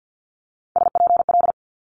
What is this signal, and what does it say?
Morse code, heard as a keyed tone, about 50 wpm, 700 Hz. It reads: SPL
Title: Multi-letter Phonemes Quick Fire 50wpm